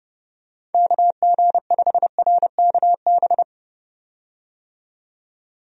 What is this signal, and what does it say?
Morse code, heard as a keyed tone, about 31 wpm, 700 Hz. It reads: KG5RKB
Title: USA Call Signs - 6 Characters - 30wpm